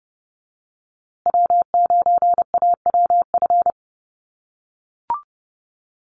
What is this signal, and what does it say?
Morse code, heard as a keyed tone, about 30 wpm, 700 Hz. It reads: W9AWF
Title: USA Call Signs Repeated 2x 30wpm